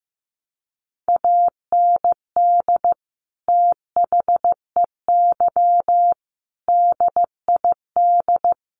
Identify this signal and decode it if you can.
Morse code, heard as a keyed tone, about 15 wpm, 700 Hz. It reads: AND THEY DID
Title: Sentences from Top 1100 Words 15wpm